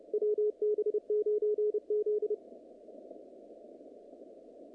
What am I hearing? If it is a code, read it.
Morse code, heard as a keyed tone, about 29 wpm, 440 Hz, with radio noise. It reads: WB9Z
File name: contests / naqp-cw-k5tr-op-k5pi- -2200